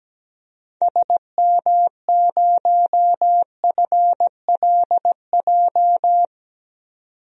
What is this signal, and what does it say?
Morse code, heard as a keyed tone, about 17 wpm, 700 Hz. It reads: SM0FLJ
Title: DXCC-100 Call Signs - 6 Characters - 17wpm